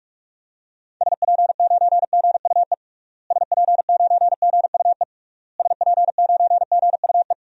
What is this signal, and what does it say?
Morse code, heard as a keyed tone, about 41 wpm, 700 Hz. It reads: SP9GUE SP9GUE SP9GUE
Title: DXCC-100 Call Signs - 6 Characters - Repeated 3x - 45wpm